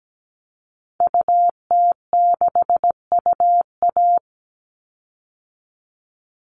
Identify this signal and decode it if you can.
Morse code, heard as a keyed tone, about 17 wpm, 700 Hz. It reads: UT6UA
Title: Call Signs Repeated 3x - 17wpm